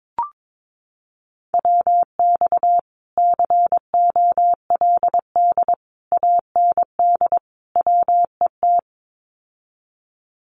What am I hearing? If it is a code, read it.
Morse code, heard as a keyed tone, about 22 wpm, 700 Hz. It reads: WX COLD AND WET
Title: Common QSO Phrases Encore 22wpm